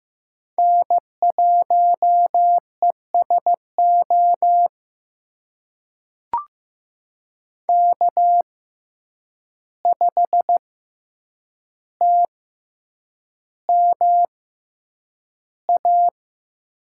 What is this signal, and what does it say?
Morse code, heard as a keyed tone, about 15 wpm, spaced as for 8 wpm, 700 Hz. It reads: N1ESO K5TMA
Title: USA Call Signs - 5 Characters - 6x Character Spacing - 15wpm